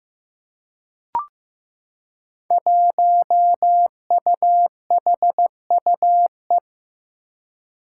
Morse code, heard as a keyed tone, about 15 wpm, 700 Hz. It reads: 1UHUE
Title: Five Challenging Letter-Number Combinations 15wpm